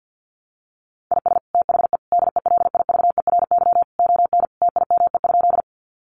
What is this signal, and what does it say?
Morse code, heard as a keyed tone, about 48 wpm, 700 Hz. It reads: IS THE DELIVERY ON TIME?
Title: Sentences from Top 1500 Words 50wpm